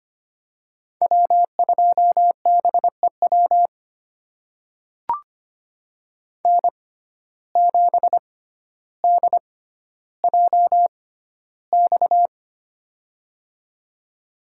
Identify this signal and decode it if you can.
Morse code, heard as a keyed tone, about 25 wpm, spaced as for 13 wpm, 700 Hz. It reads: W2BEW N7DJX
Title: USA Call Signs - 5 Characters - 6x Character Spacing - 25wpm